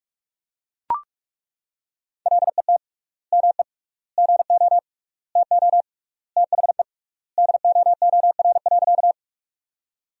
Morse code, heard as a keyed tone, about 41 wpm, 700 Hz. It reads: LET ME GO TO THE DOOR.
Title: 3x Word-Spacing - Sentences from Top 300 Words - 45wpm